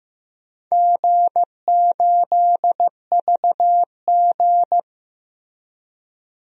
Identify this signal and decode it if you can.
Morse code, heard as a keyed tone, about 15 wpm, 700 Hz. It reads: G8VG
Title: DXCC-100 Call Signs - 4 Characters - Repeated 3x - 15wpm